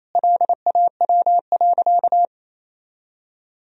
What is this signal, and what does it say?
Morse code, heard as a keyed tone, about 28 wpm, 700 Hz. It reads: LAW.
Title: Sentences from Top 1000 Words - Review Part 3 - 28wpm